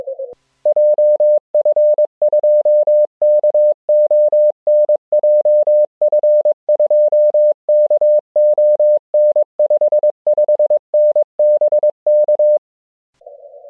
Morse code, heard as a keyed tone, about 22 wpm, 600 Hz. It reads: JF2KON JF2KON 55N BK